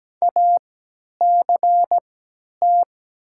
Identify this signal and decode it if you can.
Morse code, heard as a keyed tone, about 17 wpm, spaced as for 10 wpm, 700 Hz. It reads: ACT
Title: 3x Character Spacing - Top 500 Words - 17wpm